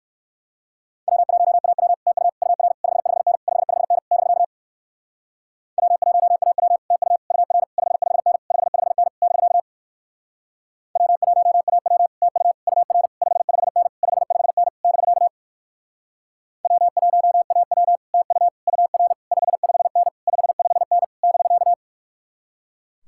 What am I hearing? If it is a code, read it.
Morse code, heard as a keyed tone, about 56 wpm, 700 Hz. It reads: W1AW TU UR 55N 55N <BK> W1AW TU UR 55N 55N <BK> W1AW TU UR 55N 55N <BK> W1AW TU UR 55N 55N <BK>
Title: Speed-Racing Common QSO Phrases - 45wpm